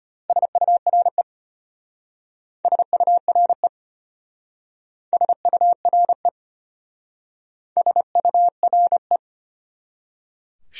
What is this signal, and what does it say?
Morse code, heard as a keyed tone, about 32 wpm, 700 Hz. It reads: SURE SURE SURE SURE